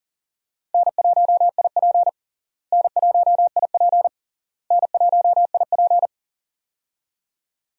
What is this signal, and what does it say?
Morse code, heard as a keyed tone, about 40 wpm, 700 Hz. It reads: N1IP N1IP N1IP
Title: USA Call Signs - 4 Characters - Repeated 3x - 40wpm